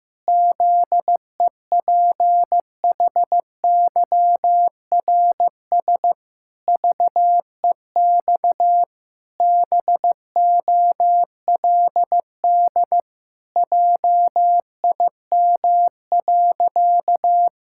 Morse code, heard as a keyed tone, about 16 wpm, 700 Hz. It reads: ZEPHYRS VEX BOLD JIM.
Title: Pangram Challenge v2 - 15wpm